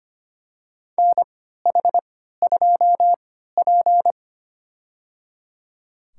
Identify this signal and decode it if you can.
Morse code, heard as a keyed tone, about 25 wpm, spaced as for 14 wpm, 700 Hz. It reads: NH2P
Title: USA Call Signs - 4 Characters - 3x Character Spacing - 25wpm